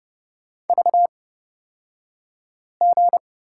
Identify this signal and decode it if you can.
Morse code, heard as a keyed tone, about 29 wpm, spaced as for 5 wpm, 700 Hz. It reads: VG
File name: Lesson 38d - Letter q - Two Characters 30wpm